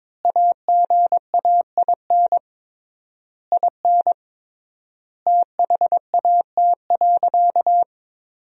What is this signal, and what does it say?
Morse code, heard as a keyed tone, about 21 wpm, 700 Hz. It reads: AGAIN IN THAT.
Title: 3x Word-Spacing - Sentences from Top 200 Words - 22wpm